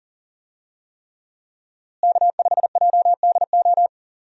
Morse code, heard as a keyed tone, about 40 wpm, 700 Hz. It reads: K5JDO
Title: USA Call Signs - 5 Characters - Repeated 3x - 40wpm